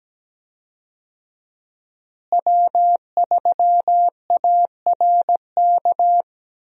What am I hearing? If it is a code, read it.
Morse code, heard as a keyed tone, about 17 wpm, 700 Hz. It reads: W3ARK
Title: USA Call Signs - 5 Characters - Repeated 2x - 17wpm